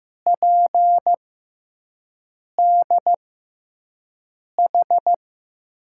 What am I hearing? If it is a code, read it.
Morse code, heard as a keyed tone, about 15 wpm, spaced as for 5 wpm, 700 Hz. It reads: PDH
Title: USA Call Signs - 5 Characters - 6x Character Spacing - 15wpm